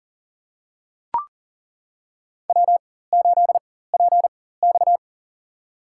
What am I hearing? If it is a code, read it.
Morse code, heard as a keyed tone, about 37 wpm, spaced as for 18 wpm, 700 Hz. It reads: W8PX
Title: USA Call Signs - 4 Characters - 4x Character Spacing - 40wpm